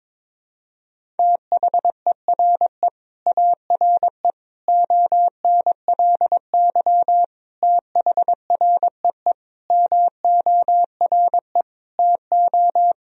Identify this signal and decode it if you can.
Morse code, heard as a keyed tone, about 22 wpm, 700 Hz. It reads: THERE ARE ONLY THREE MORE TO
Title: Sentences from Top 1000 Words - Review Part 1 - 22wpm